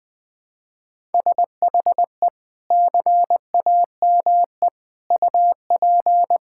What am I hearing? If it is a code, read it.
Morse code, heard as a keyed tone, about 20 wpm, 700 Hz. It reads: SHE CAME UP
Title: Sentences with 6-Word Phrases - Spoken Beginning - 20wpm